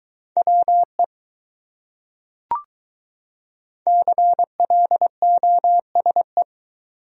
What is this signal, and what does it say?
Morse code, heard as a keyed tone, about 21 wpm, 700 Hz. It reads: WE CLOSE